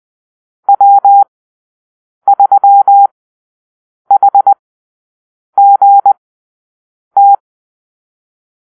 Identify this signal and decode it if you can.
Morse code, heard as a keyed tone, about 20 wpm, spaced as for 7 wpm, 800 Hz. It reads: W3HGT
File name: arrl / Call-Sign Drill-7